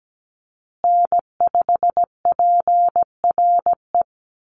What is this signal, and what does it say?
Morse code, heard as a keyed tone, about 17 wpm, 700 Hz. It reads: N5PRE
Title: USA Call Signs Repeated 2x 17wpm